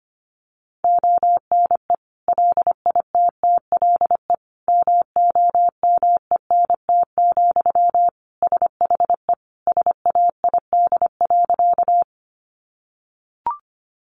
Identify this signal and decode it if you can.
Morse code, heard as a keyed tone, about 25 wpm, 700 Hz. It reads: ONE LITTLE MOMENT, SHE SAID.
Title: Sentences from Top 1000 Words 25wpm